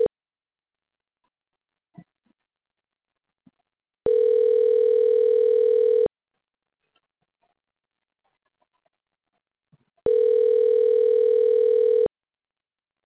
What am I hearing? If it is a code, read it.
Ringback tone (North America), 440 + 480 Hz.